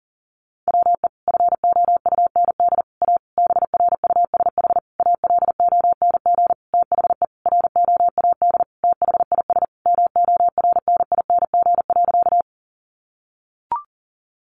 Morse code, heard as a keyed tone, about 40 wpm, 700 Hz. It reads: WE FOUND A BRUSH ALONG THE ROAD THIS MORNING.
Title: Sentences from Top 1100 Words 40wpm